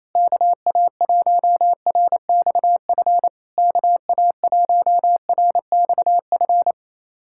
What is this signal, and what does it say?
Morse code, heard as a keyed tone, about 28 wpm, 690 Hz. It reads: KA1RXF KA1RXF
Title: SOTA QSO Practice 28wpm